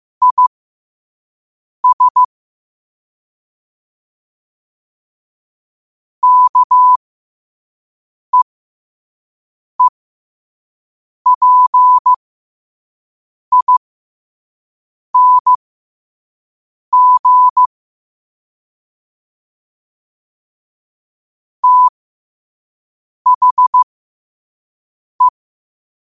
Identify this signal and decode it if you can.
Morse code, heard as a keyed tone, about 15 wpm, spaced as for 5 wpm, 1000 Hz. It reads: IS KEEPING THE
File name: 5wpm